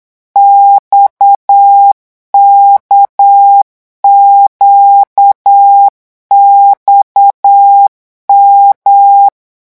Morse code, heard as a keyed tone, about 8 wpm, 800 Hz. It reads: XKQXM
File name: morse mp3 / revision3-vitesse 8 mots